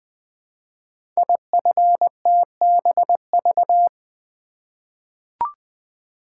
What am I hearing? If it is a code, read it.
Morse code, heard as a keyed tone, about 20 wpm, 700 Hz. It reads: IFTBV
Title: Code Groups - 5 Letters - 20wpm